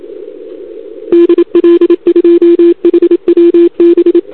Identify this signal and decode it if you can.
Morse code, heard as a keyed tone, about 27 wpm, 340 Hz. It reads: DL2HWB
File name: CQWWCW2021 / UA0AGI